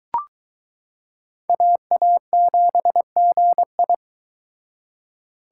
Morse code, heard as a keyed tone, about 22 wpm, 700 Hz. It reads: AA7GI